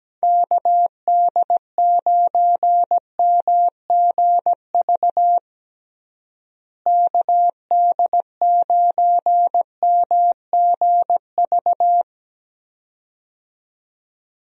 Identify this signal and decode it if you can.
Morse code, heard as a keyed tone, about 17 wpm, 700 Hz. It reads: KD9MGV KD9MGV
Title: USA Call Signs - 6 Characters - Repeated 2x - 17wpm